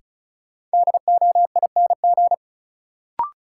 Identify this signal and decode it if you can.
Morse code, heard as a keyed tone, about 36 wpm, 700 Hz. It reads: DOING
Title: Sets of 4 Words - New Word Drill - 35wpm